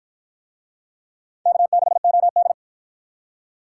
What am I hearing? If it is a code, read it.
Morse code, heard as a keyed tone, about 53 wpm, 700 Hz. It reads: K6OD